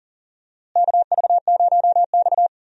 Morse code, heard as a keyed tone, about 39 wpm, 700 Hz. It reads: KV0X
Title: USA Call Signs - 4 Characters - Repeated 2x - 40wpm